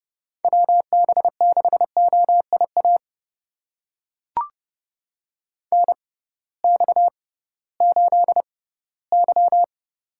Morse code, heard as a keyed tone, about 30 wpm, spaced as for 18 wpm, 700 Hz. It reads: WB6OIA NX8Y
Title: USA Call Signs - 6x Character Spacing - 30wpm